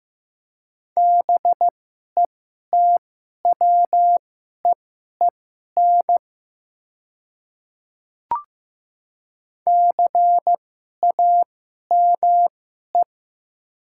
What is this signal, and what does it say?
Morse code, heard as a keyed tone, about 15 wpm, spaced as for 11 wpm, 700 Hz. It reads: BETWEEN CAME
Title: 2x Character Spacing - Top 300 Words - 15wpm